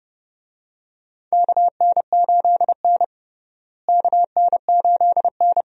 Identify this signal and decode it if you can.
Morse code, heard as a keyed tone, about 30 wpm, 700 Hz. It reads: KN8N KN8N
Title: USA Call Signs - 4 Characters - Repeated 2x - 30wpm